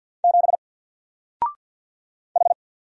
Morse code, heard as a keyed tone, about 50 wpm, 700 Hz. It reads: 7 H
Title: Single Letter-Number Alt 50wpm